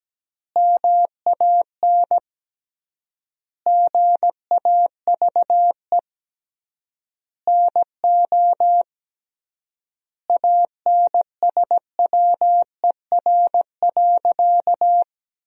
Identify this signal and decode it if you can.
Morse code, heard as a keyed tone, about 17 wpm, 700 Hz. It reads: MAN GAVE NO ANSWER.
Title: 3x Word-Spacing - Sentences from Top 500 Words - 17wpm